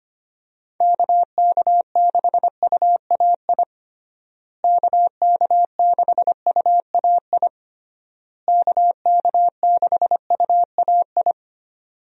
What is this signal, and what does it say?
Morse code, heard as a keyed tone, about 25 wpm, 700 Hz. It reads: KK6UAI KK6UAI KK6UAI
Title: USA Call Signs - 6 Characters - Repeated 3x - 25wpm